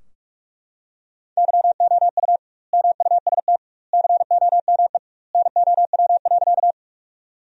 Morse code, heard as a keyed tone, about 43 wpm, 700 Hz. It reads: YOU MUST COME NOW.
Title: 2x Word-Spacing - Sentences from Top 200 Words - 45wpm